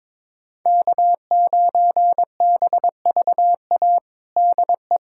Morse code, heard as a keyed tone, about 22 wpm, 700 Hz. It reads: K9BVA DE
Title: USA Callsign DE Callsign 22wpm